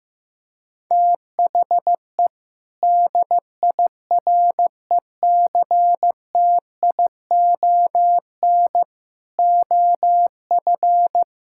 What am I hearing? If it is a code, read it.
Morse code, heard as a keyed tone, about 15 wpm, 700 Hz. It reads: THE DIRECTION OF
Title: Sentences from Top 1000 Words - Review Part 5 - 15wpm